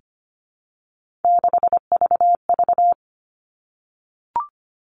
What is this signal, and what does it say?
Morse code, heard as a keyed tone, about 25 wpm, 700 Hz. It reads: 6VV
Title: All Cut Numbers - 3-Digit 25wpm